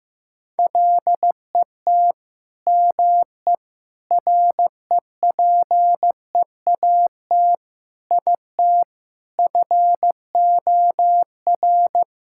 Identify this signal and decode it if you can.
Morse code, heard as a keyed tone, about 16 wpm, 700 Hz. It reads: LET ME REPEAT IT FOR
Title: Sentences from Top 1600 Words 15wpm